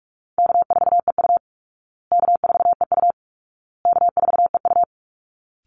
Morse code, heard as a keyed tone, about 44 wpm, 700 Hz. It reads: K4EU K4EU K4EU
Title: Easy Call Signs Repeated 3x - 45wpm